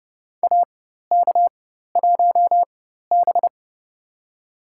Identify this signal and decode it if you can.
Morse code, heard as a keyed tone, about 29 wpm, spaced as for 14 wpm, 700 Hz. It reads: AK1B
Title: USA Call Signs - 4 Characters - 4x Character Spacing - 30wpm